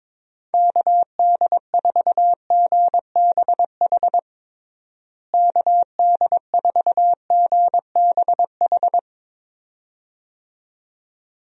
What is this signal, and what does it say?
Morse code, heard as a keyed tone, about 22 wpm, 700 Hz. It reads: KD4GBH KD4GBH
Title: USA Call Signs - 6 Characters - Repeated 2x - 22wpm